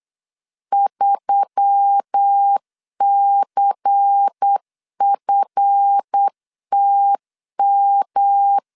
Morse code, heard as a keyed tone, about 8 wpm, 800 Hz. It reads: 3CFTM
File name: LEZIONI / 7 GF9Z / 7 GF9 8wpm 3545